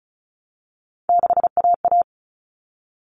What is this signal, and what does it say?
Morse code, heard as a keyed tone, about 34 wpm, 700 Hz. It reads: 6AA
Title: All Cut Numbers - 3-Digit 35wpm